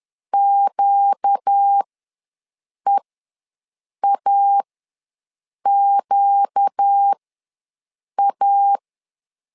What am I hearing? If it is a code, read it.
Morse code, heard as a keyed tone, about 10 wpm, spaced as for 6 wpm, 800 Hz. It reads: QEAQA
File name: LEZIONI / 4 Q ws 10wpm 2306